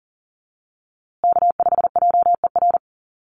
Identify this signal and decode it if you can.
Morse code, heard as a keyed tone, about 40 wpm, 700 Hz. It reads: K5JER
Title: USA Call Signs Repeated 3x 40wpm